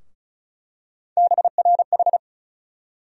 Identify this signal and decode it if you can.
Morse code, heard as a keyed tone, about 36 wpm, 700 Hz. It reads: BRH
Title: USA Call Sign Postfixes - 3 Letters 35wpm